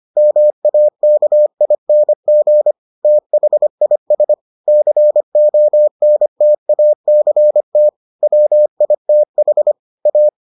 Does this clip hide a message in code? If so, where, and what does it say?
Yes: Morse code from 0.2 to 10.3 s, heard as a keyed tone, about 25 wpm, 600 Hz. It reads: MAKING THIS CONTACT WITH A